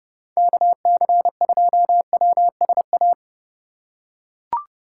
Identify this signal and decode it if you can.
Morse code, heard as a keyed tone, about 30 wpm, 700 Hz. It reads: KC2WSA
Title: USA Call Signs - 6 Characters - 8x Character Spacing - 30wpm